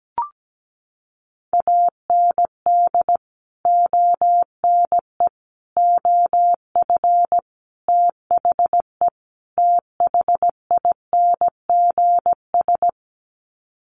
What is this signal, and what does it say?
Morse code, heard as a keyed tone, about 17 wpm, 700 Hz. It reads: AND ONE OF THE THINGS
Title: Sets of 5 Words 17wpm